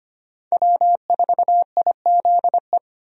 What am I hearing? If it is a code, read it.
Morse code, heard as a keyed tone, about 25 wpm, 700 Hz. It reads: W4IZE
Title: USA Call Signs - 5 Characters - 25wpm